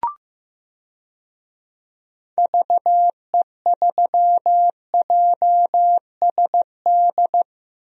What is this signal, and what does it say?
Morse code, heard as a keyed tone, about 15 wpm, 700 Hz. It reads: VE3JSD
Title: DXCC-100 Call Signs - 6 Characters - Repeated 3x - 15wpm